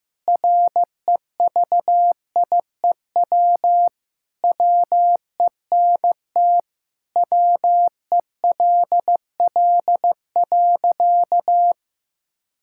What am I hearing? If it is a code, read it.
Morse code, heard as a keyed tone, about 15 wpm, 700 Hz. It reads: REVIEW WENT WELL.
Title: Sentences from Top 1000 Words - Review Part 6 - 15wpm